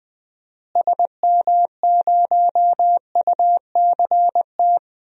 Morse code, heard as a keyed tone, about 20 wpm, 700 Hz. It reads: SM0UCT
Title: DXCC-100 Call Signs - 6 Characters - Repeated 3x - 20wpm